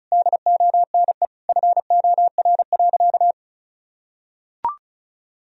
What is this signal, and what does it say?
Morse code, heard as a keyed tone, about 34 wpm, 700 Hz. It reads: DONE FOR.
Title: Sentences from Top 1000 Words - Review Part 4 - 35wpm